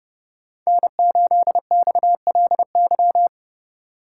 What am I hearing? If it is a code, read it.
Morse code, heard as a keyed tone, about 29 wpm, 700 Hz. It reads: N8XLY
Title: Speed Racing - USA 1x3 Call Signs 25wpm